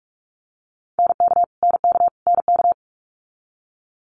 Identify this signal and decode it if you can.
Morse code, heard as a keyed tone, about 46 wpm, 700 Hz. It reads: NK NK NK
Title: Multi-letter Phonemes Repeated 3x 45wpm